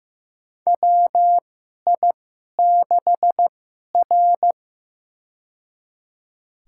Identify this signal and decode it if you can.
Morse code, heard as a keyed tone, about 15 wpm, spaced as for 11 wpm, 700 Hz. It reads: WI6R
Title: USA Call Signs - 4 Characters - 2x Character Spacing - 15wpm